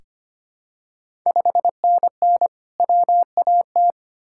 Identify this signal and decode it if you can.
Morse code, heard as a keyed tone, about 25 wpm, 700 Hz. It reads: 5NN WAT
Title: Ontario QSO Party Exchange 25wpm